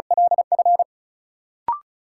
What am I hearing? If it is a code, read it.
Morse code, heard as a keyed tone, about 36 wpm, 700 Hz. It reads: LF
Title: Sentences from Top 1000 Words - Review Part 1 - New Word Drill - 35wpm